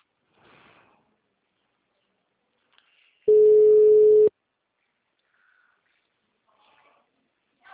Ringback tone (Europe, 425 Hz).